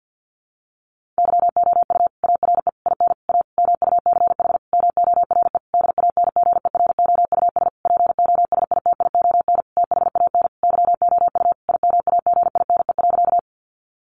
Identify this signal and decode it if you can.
Morse code, heard as a keyed tone, about 50 wpm, 700 Hz. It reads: YOU ARE IN A MUCH MORE DANGEROUS POSITION THAN YOU IMAGINE.
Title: Sentences from Top 1400 Words 50wpm